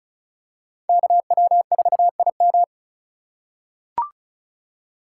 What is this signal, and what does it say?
Morse code, heard as a keyed tone, about 36 wpm, 700 Hz. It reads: KW4IM
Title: USA Call Signs - 3x Character Spacing - 35wpm